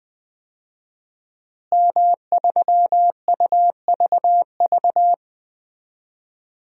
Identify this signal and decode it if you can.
Morse code, heard as a keyed tone, about 20 wpm, 700 Hz. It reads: M3UVV
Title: DXCC-100 Call Signs - 5 Characters - Repeated 3x - 20wpm